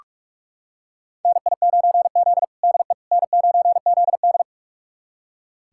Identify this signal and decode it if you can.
Morse code, heard as a keyed tone, about 46 wpm, 700 Hz. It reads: NI9Z DE N9ZD
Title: USA Callsign DE Callsign - 4 Characters - 45wpm